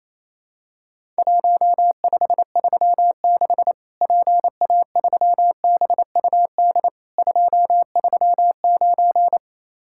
Morse code, heard as a keyed tone, about 28 wpm, 700 Hz. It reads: 1536 PA3BUD 239
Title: Worked All Europe DX Contest - QTC Exchange - 28wpm